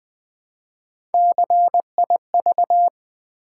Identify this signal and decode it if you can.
Morse code, heard as a keyed tone, about 20 wpm, 700 Hz. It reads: CIV
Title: Build A Word - 1 Letter At A Time - 3 to 5-Letter Words - 20wpm